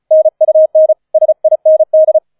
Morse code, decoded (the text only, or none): NUN SIND